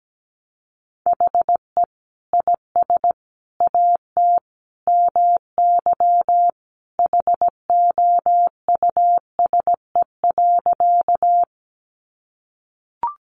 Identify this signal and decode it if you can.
Morse code, heard as a keyed tone, about 17 wpm, 700 Hz. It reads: HE IS AT MY HOUSE.
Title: Sentences from Top 200 Words 17wpm